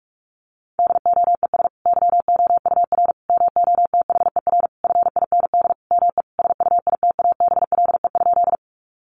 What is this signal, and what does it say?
Morse code, heard as a keyed tone, about 44 wpm, 700 Hz. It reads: DOES YOUR MOTHER FIND ME SUITABLE?
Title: Sentences from Top 1400 Words 45wpm